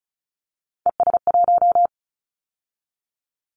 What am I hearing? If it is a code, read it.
Morse code, heard as a keyed tone, about 36 wpm, 700 Hz. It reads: ES1
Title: Callsign Prefixes for Top 50 DXCC - 35wpm